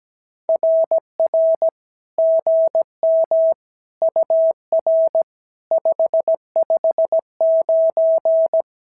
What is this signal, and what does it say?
Morse code, heard as a keyed tone, about 17 wpm, 650 Hz. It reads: RR GM UR 559
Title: SOTA QSO Practice 17wpm